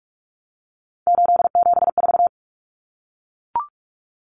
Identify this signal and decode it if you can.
Morse code, heard as a keyed tone, about 44 wpm, 700 Hz. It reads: 874
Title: Numbers - 3-Digit 45wpm